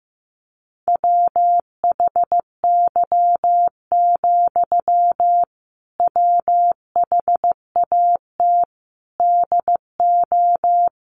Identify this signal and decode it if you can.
Morse code, heard as a keyed tone, about 14 wpm, 700 Hz. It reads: WHY, WHAT DO
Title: Sentences from Top 200 Words 15wpm